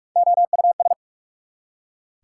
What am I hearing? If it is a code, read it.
Morse code, heard as a keyed tone, about 42 wpm, 690 Hz. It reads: OUS
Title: Prefixes and Suffixes 45wpm